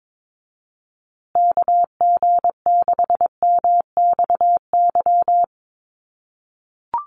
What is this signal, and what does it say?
Morse code, heard as a keyed tone, about 22 wpm, 700 Hz. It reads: KG6MXY
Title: USA Call Signs Repeated 3x 22wpm